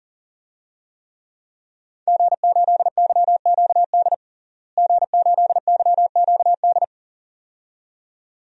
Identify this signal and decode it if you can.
Morse code, heard as a keyed tone, about 40 wpm, 700 Hz. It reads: G8YQD G8YQD
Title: DXCC-100 Call Signs - 5 Characters - Repeated 2x - 40wpm